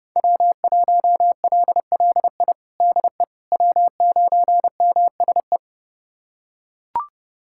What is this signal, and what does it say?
Morse code, heard as a keyed tone, about 30 wpm, 700 Hz. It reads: W1LLI DE W9MSE
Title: USA Callsign DE Callsign - 5 Characters - 30wpm